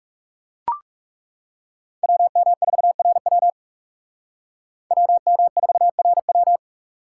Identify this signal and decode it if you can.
Morse code, heard as a keyed tone, about 38 wpm, 700 Hz. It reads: WM4RW WM4RW